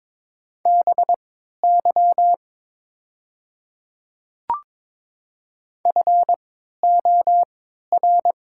Morse code, heard as a keyed tone, about 22 wpm, spaced as for 12 wpm, 700 Hz. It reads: BY FOR
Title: 3x Character Spacing - Top 100 Words - 22wpm